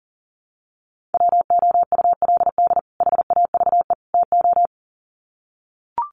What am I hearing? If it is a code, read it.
Morse code, heard as a keyed tone, about 37 wpm, 700 Hz. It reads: WOULD HAVE TO
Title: Sets of 3 Words 40wpm